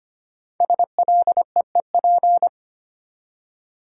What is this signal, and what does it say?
Morse code, heard as a keyed tone, about 25 wpm, 700 Hz. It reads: SLEEP
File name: 5 Letter Words - First 3 Letters Given - 25wpm